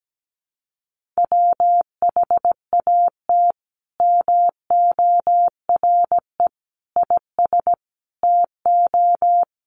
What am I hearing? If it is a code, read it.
Morse code, heard as a keyed tone, about 17 wpm, 700 Hz. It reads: WHAT MORE IS TO
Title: Sentences from Top 500 Words 17wpm